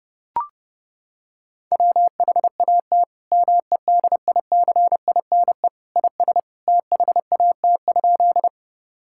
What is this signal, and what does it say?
Morse code, heard as a keyed tone, about 32 wpm, 700 Hz. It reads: WHAT MEDICINE IS THAT?
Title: Sentences from Top 1000 Words - Review Part 6 - 30wpm